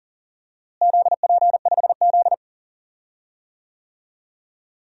Morse code, heard as a keyed tone, about 40 wpm, 700 Hz. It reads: ZP5Z
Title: DXCC-100 Call Signs - 4 Characters - Repeated 3x - 40wpm